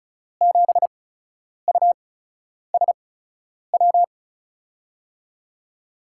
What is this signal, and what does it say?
Morse code, heard as a keyed tone, about 36 wpm, spaced as for 10 wpm, 700 Hz. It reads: 7USW
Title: USA Call Signs - 6 Characters - 8x Character Spacing - 35wpm